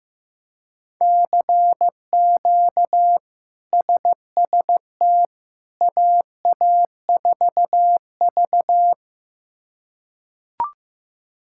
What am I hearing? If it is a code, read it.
Morse code, heard as a keyed tone, about 15 wpm, 700 Hz. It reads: CQ SST AA4V
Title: K1USN SST QSO Practice - 15wpm